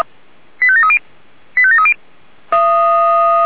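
Five-tone selective call (ZVEI), heard as quick runs of five tones, tones of about 80 ms. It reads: 85730 85730
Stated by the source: ZVEI five-tone selective call received on SDR.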